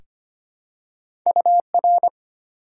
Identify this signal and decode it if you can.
Morse code, heard as a keyed tone, about 25 wpm, 700 Hz. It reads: UR
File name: Top 90 QSO Elements - 3x Character Spacing - 25wpm